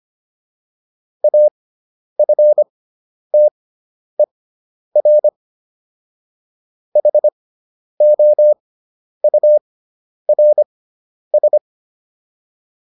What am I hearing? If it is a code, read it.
Morse code, heard as a keyed tone, about 25 wpm, spaced as for 10 wpm, 600 Hz. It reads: AFTER HOURS